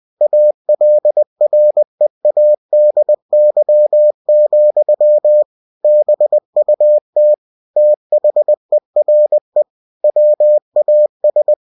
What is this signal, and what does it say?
Morse code, heard as a keyed tone, about 20 wpm, 600 Hz. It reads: ALREADY, BUT THERE WAS
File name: cw-books / winnie-the-pooh / winnie-20wpm-page-0035